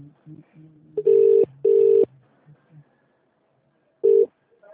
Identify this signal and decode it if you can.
Ringback tone (UK), 400 + 450 Hz, with speech.